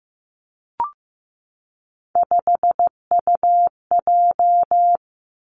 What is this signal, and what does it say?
Morse code, heard as a keyed tone, about 15 wpm, 700 Hz. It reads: 5UJ
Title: Three Challenging Letter-Number Combinations 15wpm